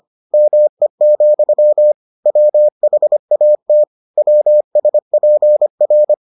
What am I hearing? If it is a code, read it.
Morse code, heard as a keyed tone, about 25 wpm, 600 Hz. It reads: ME, WHAT WSPR